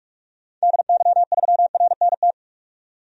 Morse code, heard as a keyed tone, about 43 wpm, 700 Hz. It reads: DY3RNT
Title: DXCC-100 Call Signs - 6 Characters - 45wpm